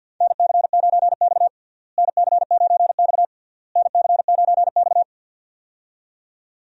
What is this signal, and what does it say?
Morse code, heard as a keyed tone, about 52 wpm, 700 Hz. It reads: NC9X NC9X NC9X
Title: USA Call Signs - 4 Characters - Repeated 3x - 50wpm